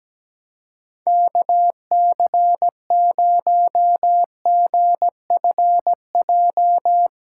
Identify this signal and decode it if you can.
Morse code, heard as a keyed tone, about 17 wpm, 700 Hz. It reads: KC0GFJ
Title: USA Call Signs - 6 Characters - 3x Character Spacing - 17wpm